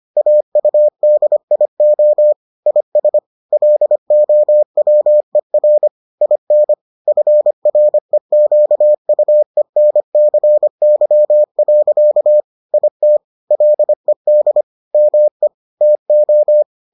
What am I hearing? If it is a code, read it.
Morse code, heard as a keyed tone, about 25 wpm, 600 Hz. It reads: AUDIO IS LOWER IN FREQUENCY. IT LED ME TO